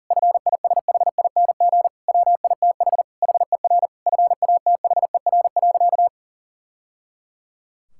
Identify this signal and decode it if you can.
Morse code, heard as a keyed tone, about 40 wpm, 700 Hz. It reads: FISHING WITH HER FATHER.
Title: Sentences from Top 1000 Words - Review Part 5 - 40wpm